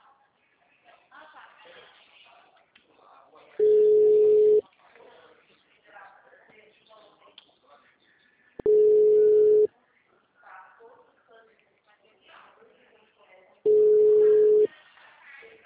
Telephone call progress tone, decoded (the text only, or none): ringback tone